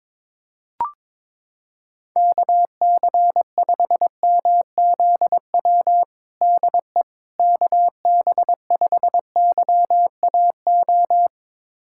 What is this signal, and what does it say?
Morse code, heard as a keyed tone, about 22 wpm, 700 Hz. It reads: KC5MZW DE KB5YAO
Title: USA Callsign DE Callsign - 6 Characters - 22wpm